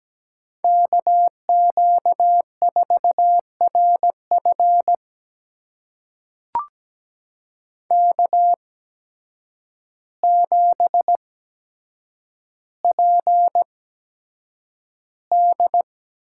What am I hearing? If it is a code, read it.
Morse code, heard as a keyed tone, about 17 wpm, spaced as for 8 wpm, 700 Hz. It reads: KQ4RF K7PD
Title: USA Call Signs - 5 Characters - 8x Character Spacing - 17wpm